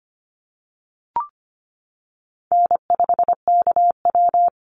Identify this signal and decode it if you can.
Morse code, heard as a keyed tone, about 25 wpm, 700 Hz. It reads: N5KW
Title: Easy Call Signs - 25wpm